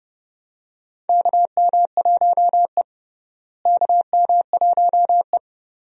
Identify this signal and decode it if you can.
Morse code, heard as a keyed tone, about 32 wpm, 700 Hz. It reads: KM1E KM1E
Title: USA Call Signs - 4 Characters - Repeated 2x - 30wpm